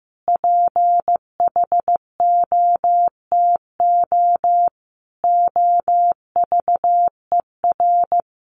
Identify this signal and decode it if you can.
Morse code, heard as a keyed tone, about 15 wpm, 700 Hz. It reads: PHOTO OVER
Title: Sentences from Top 1200 Words 15wpm